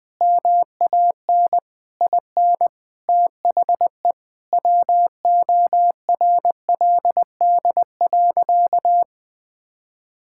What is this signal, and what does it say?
Morse code, heard as a keyed tone, about 20 wpm, 700 Hz. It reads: MAN IN THE WORLD.
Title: Sentences with 7-Word Phrases - 20wpm